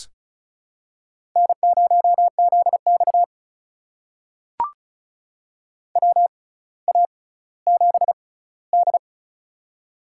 Morse code, heard as a keyed tone, about 36 wpm, spaced as for 18 wpm, 700 Hz. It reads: N0ZX WA7D
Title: USA Call Signs - 4 Characters - 6x Character Spacing - 35wpm